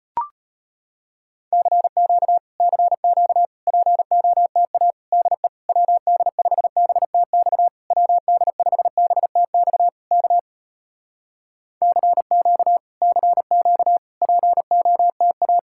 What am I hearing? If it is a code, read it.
Morse code, heard as a keyed tone, about 34 wpm, 700 Hz. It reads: CQ CQ POTA DE WD5BTX WD5BTX K CQ CQ POTA